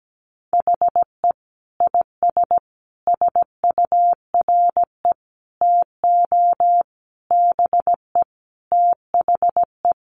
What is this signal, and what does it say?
Morse code, heard as a keyed tone, about 17 wpm, 700 Hz. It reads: HE IS SURE TO BE THE
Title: Sentences from Top 1400 Words 17wpm